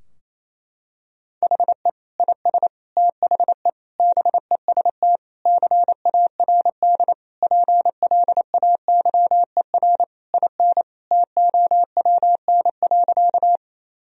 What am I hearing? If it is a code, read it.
Morse code, heard as a keyed tone, about 28 wpm, 700 Hz. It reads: HE IS THE BEST CARD PLAYER IN TOWN.
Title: Sentences from Top 1000 Words - Review Part 4 - 28wpm